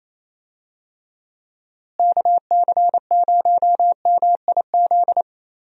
Morse code, heard as a keyed tone, about 28 wpm, 700 Hz. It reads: KC0MIZ
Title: USA Call Signs - 6 Characters - Repeated 2x - 28wpm